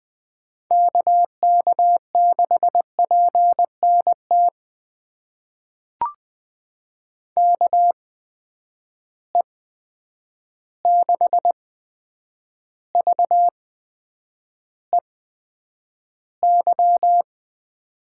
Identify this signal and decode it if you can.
Morse code, heard as a keyed tone, about 20 wpm, spaced as for 9 wpm, 700 Hz. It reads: KK6PNT KE6VEY
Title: USA Call Signs - 6 Characters - 8x Character Spacing - 20wpm